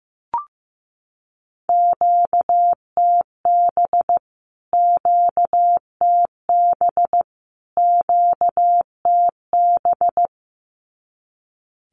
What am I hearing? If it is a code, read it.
Morse code, heard as a keyed tone, about 14 wpm, 700 Hz. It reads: QTB QTB QTB
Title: Q-Codes Repeated 3x 15wpm